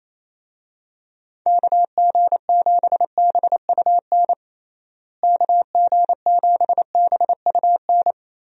Morse code, heard as a keyed tone, about 28 wpm, 700 Hz. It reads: KG7BUN KG7BUN
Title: USA Call Signs - 6 Characters - Repeated 2x - 28wpm